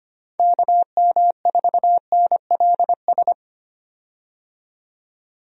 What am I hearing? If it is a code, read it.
Morse code, heard as a keyed tone, about 25 wpm, 700 Hz. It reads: KM4NLS
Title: USA Call Signs - 6 Characters - 25wpm